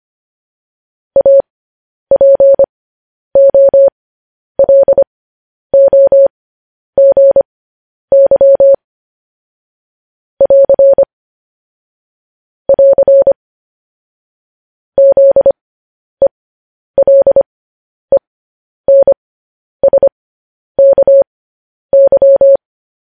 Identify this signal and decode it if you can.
Morse code, heard as a keyed tone, about 25 wpm, spaced as for 10 wpm, 550 Hz. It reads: APOLOGY <AR> <AR> ZELENSKY
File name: podcast / News headlines-1x-25x10